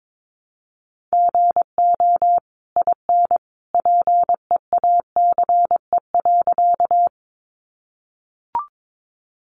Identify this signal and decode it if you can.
Morse code, heard as a keyed tone, about 22 wpm, 700 Hz. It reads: GO IN PEACE.
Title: Sentences from Top 1500 Words 22wpm